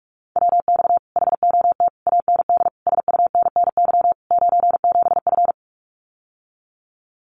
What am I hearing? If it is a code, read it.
Morse code, heard as a keyed tone, about 46 wpm, 700 Hz. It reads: WX HOT AND SUNNY 97F
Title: Common QSO Phrases Encore 45wpm